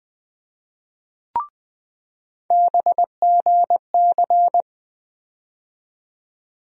Morse code, heard as a keyed tone, about 20 wpm, 700 Hz. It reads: BGC
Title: USA Call Sign Postfixes - 3 Letters 20wpm